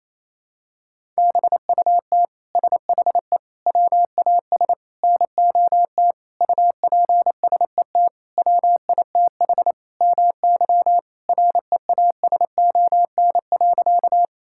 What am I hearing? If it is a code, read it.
Morse code, heard as a keyed tone, about 28 wpm, 700 Hz. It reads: BUT SHE WAS NOT UPSET WITH MY REASON.
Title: Sentences from Top 1000 Words - Review Part 5 - 28wpm